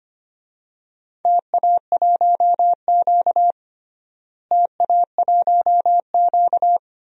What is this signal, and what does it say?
Morse code, heard as a keyed tone, about 25 wpm, 700 Hz. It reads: TA1Q TA1Q
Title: DXCC-100 Call Signs - 4 Characters - Repeated 2x - 25wpm